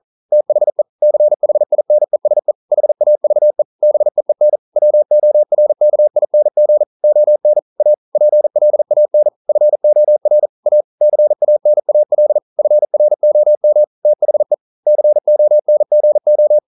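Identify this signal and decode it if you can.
Morse code, heard as a keyed tone, about 39 wpm, 600 Hz. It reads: THE CHINESE HAVE BEEN WORKING ON A PLAN FOR A CANAL FROM THE CONGO
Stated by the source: Morse code at 40 wpm, practice text, CW med Gnister.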